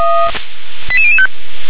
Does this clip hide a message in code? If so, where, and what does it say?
Yes: five-tone selective call (ZVEI) from 0.9 to 1.3 s, heard as quick runs of five tones, tones of about 80 ms. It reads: 88005